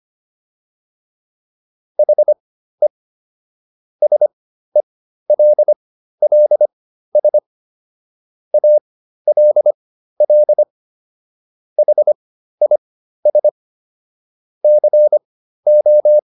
Morse code, heard as a keyed tone, about 25 wpm, spaced as for 13 wpm, 600 Hz. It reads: HE SELLS ALL HIS CO